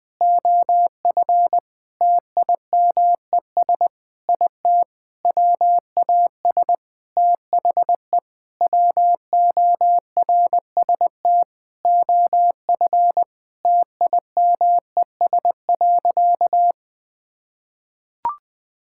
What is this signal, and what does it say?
Morse code, heard as a keyed tone, about 21 wpm, 700 Hz. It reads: OF TIMES IT WAS THE WORST OF TIMES.
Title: Sentences with 8-Word Phrases - 20wpm